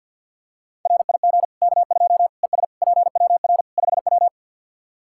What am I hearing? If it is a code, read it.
Morse code, heard as a keyed tone, about 49 wpm, 700 Hz. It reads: RIG K2 ES PWR 5W
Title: Practice QSO - Exchange 2 - Info - 50wpm